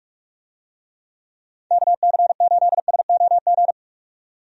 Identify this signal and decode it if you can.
Morse code, heard as a keyed tone, about 46 wpm, 700 Hz. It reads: KC8SOG
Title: USA Call Signs - 6 Characters - 45wpm